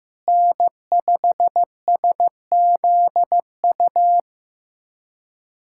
Morse code, heard as a keyed tone, about 15 wpm, 700 Hz. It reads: N5SZU
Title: USA Callsign DE Callsign - 5 Characters - 15wpm